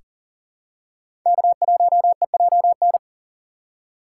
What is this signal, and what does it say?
Morse code, heard as a keyed tone, about 40 wpm, 700 Hz. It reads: K1EJN
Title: USA Call Signs - 5 Characters - 6x Character Spacing - 40wpm